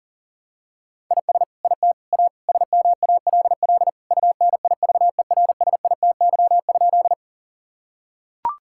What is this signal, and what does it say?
Morse code, heard as a keyed tone, about 40 wpm, 700 Hz. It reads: IS IT A SMALL UNIVERSITY?
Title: Sentences from Top 1000 Words - Review Part 5 - 40wpm